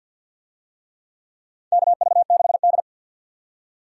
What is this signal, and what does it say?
Morse code, heard as a keyed tone, about 53 wpm, 700 Hz. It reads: KV6D
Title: USA Call Signs - 4 Characters - 50wpm